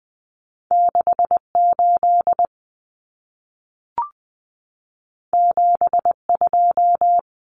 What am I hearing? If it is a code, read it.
Morse code, heard as a keyed tone, about 20 wpm, 700 Hz. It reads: 68 72
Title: Common Cut Numbers - 2-Digit 20wpm